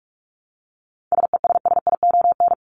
Morse code, heard as a keyed tone, about 44 wpm, 700 Hz. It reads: SESSION
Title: Top 1501-1750 Words 45wpm